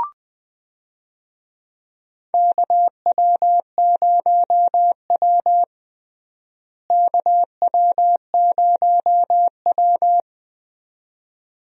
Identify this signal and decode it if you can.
Morse code, heard as a keyed tone, about 20 wpm, 700 Hz. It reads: KW0W KW0W
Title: USA Call Signs - 4 Characters - Repeated 2x - 20wpm